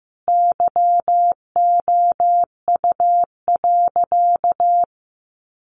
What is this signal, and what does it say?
Morse code, heard as a keyed tone, about 15 wpm, 700 Hz. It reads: YOU.
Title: Sentences from Top 500 Words 15wpm